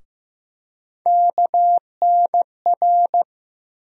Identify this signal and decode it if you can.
Morse code, heard as a keyed tone, about 15 wpm, 700 Hz. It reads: KNR
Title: USA Call Sign Postfixes - 3 Letters 15wpm